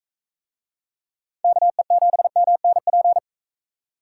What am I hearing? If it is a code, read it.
Morse code, heard as a keyed tone, about 44 wpm, 700 Hz. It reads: KE7MNP